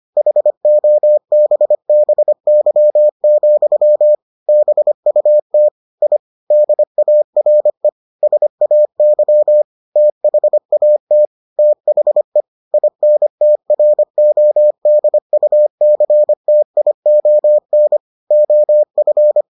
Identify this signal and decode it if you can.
Morse code, heard as a keyed tone, about 25 wpm, 600 Hz. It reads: HOBBY, BUT I DARE SAY THAT THE INTRODUCTION OF